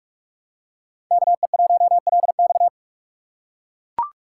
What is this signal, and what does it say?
Morse code, heard as a keyed tone, about 42 wpm, 700 Hz. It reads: KE1LX
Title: USA Call Signs - 5 Characters - 6x Character Spacing - 45wpm